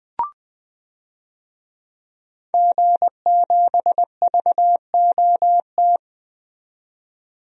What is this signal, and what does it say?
Morse code, heard as a keyed tone, about 20 wpm, 700 Hz. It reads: G7VOT
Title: DXCC-100 Call Signs - 5 Characters - 20wpm